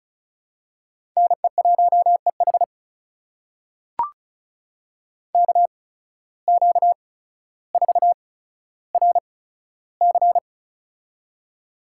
Morse code, heard as a keyed tone, about 34 wpm, spaced as for 15 wpm, 700 Hz. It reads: NE1EH KQ4RC
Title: USA Call Signs - 5 Characters - 8x Character Spacing - 35wpm